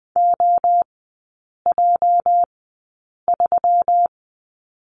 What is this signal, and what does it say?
Morse code, heard as a keyed tone, about 20 wpm, spaced as for 8 wpm, 700 Hz. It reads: OJ3
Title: 212 Lesson 34e - Number 8 - Three Characters 20wpm